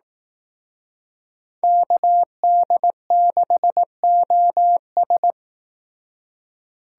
Morse code, heard as a keyed tone, about 18 wpm, 700 Hz. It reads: KD6OS
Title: Speed Racing - USA 2x2 Call Signs 15wpm